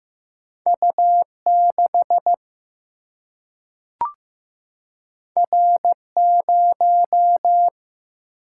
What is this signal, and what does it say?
Morse code, heard as a keyed tone, about 15 wpm, 700 Hz. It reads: U 6 R 0
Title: Single-Letter Single-Number 15wpm